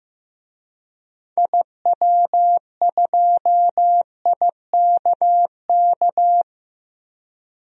Morse code, heard as a keyed tone, about 15 wpm, 700 Hz. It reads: IW2IKK
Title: DXCC-100 Call Signs - 6 Characters - Repeated 2x - 15wpm